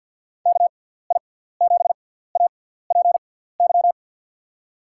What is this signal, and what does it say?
Morse code, heard as a keyed tone, about 46 wpm, spaced as for 17 wpm, 700 Hz. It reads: KI7APY
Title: USA Call Signs - 6x Character Spacing - 50wpm